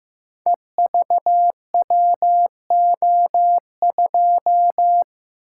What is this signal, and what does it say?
Morse code, heard as a keyed tone, about 15 wpm, 700 Hz. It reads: EVWO2
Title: Code Groups - 5 Letter:Numbers - 15wpm